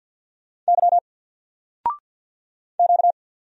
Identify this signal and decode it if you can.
Morse code, heard as a keyed tone, about 50 wpm, 700 Hz. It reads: Y Q
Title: Single Letter-Number 50wpm